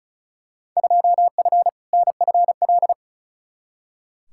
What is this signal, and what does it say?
Morse code, heard as a keyed tone, about 34 wpm, 700 Hz. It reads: 2F NFL
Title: Field Day Exchanges - 35wpm